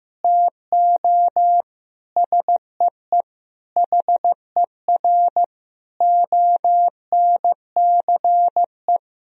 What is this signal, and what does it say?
Morse code, heard as a keyed tone, about 15 wpm, 700 Hz. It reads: TO SEE HER ONCE
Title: Sentences from Top 500 Words and Abbreviations 15wpm